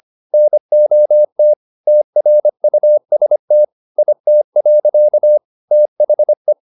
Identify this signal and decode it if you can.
Morse code, heard as a keyed tone, about 25 wpm, 600 Hz. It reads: NOT TRUST IT. THE